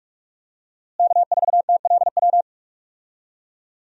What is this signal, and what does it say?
Morse code, heard as a keyed tone, about 43 wpm, 700 Hz. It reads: K4TLW
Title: USA Call Signs - 5 Characters - Repeated 2x - 45wpm